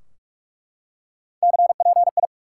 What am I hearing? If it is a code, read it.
Morse code, heard as a keyed tone, about 48 wpm, 700 Hz. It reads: CPI
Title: Top 90 QSO Elements 45wpm